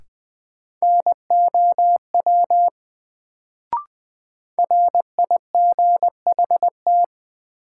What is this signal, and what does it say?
Morse code, heard as a keyed tone, about 21 wpm, 700 Hz. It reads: NOW RIGHT
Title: Sets of 2 Words - New Word Drill - 20wpm